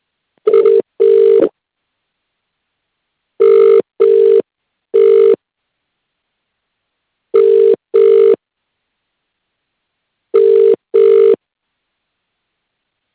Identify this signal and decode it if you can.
Ringback tone (UK), 400 + 450 Hz.